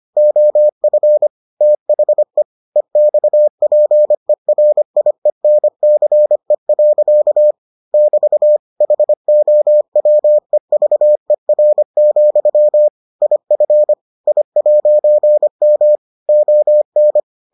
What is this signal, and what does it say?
Morse code, heard as a keyed tone, about 25 wpm, 600 Hz. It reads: OF THE EXPERIENCE. <BT> HOWEVER, IF I'M ON